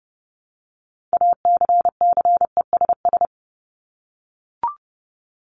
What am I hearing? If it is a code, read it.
Morse code, heard as a keyed tone, about 30 wpm, 700 Hz. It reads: ACCESS
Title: Top 501-750 Words 30wpm